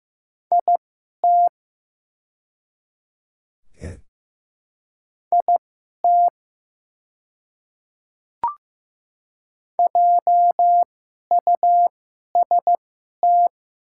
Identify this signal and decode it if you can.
Morse code, heard as a keyed tone, about 15 wpm, spaced as for 11 wpm, 700 Hz. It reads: IT IT JUST
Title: 2x Character Spacing - Top 500 Words - 15wpm